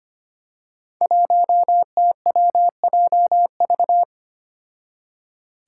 Morse code, heard as a keyed tone, about 25 wpm, 700 Hz. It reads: 1TWJV
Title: Five Challenging Letter-Number Combinations 25wpm